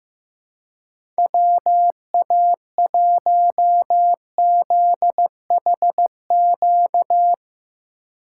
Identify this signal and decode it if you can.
Morse code, heard as a keyed tone, about 15 wpm, 700 Hz. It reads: WA1ZHQ
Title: USA Call Signs - 6 Characters - 2x Character Spacing - 15wpm